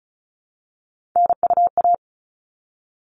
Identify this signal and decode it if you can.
Morse code, heard as a keyed tone, about 34 wpm, 700 Hz. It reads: NUA
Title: All Cut Numbers - 3-Digit 35wpm